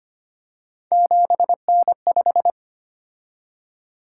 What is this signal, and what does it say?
Morse code, heard as a keyed tone, about 25 wpm, 700 Hz. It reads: 7N5
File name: Three Letter-Number Combinations - 25wpm